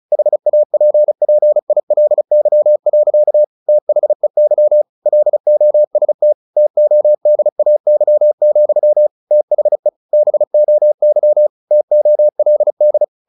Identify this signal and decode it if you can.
Morse code, heard as a keyed tone, about 34 wpm, 600 Hz. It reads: HAPPILY. THEY LOST TODAY, THE BOY TOLD
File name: the old man and the sea / the old man and the sea 1 35wpm